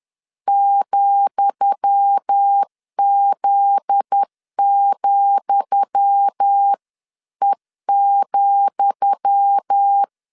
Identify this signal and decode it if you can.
Morse code, heard as a keyed tone, about 10 wpm, 800 Hz. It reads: ,Z,E,
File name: LEZIONI / 10 SIGNS / 10D COMA 10wpm 3013